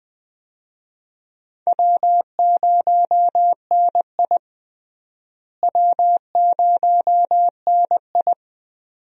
Morse code, heard as a keyed tone, about 20 wpm, 700 Hz. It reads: W0NI W0NI
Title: USA Call Signs - 4 Characters - Repeated 2x - 20wpm